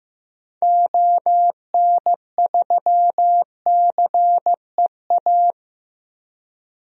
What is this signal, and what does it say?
Morse code, heard as a keyed tone, about 15 wpm, 700 Hz. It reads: ON3CEA
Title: DXCC-100 Call Signs - 6 Characters - Repeated 3x - 15wpm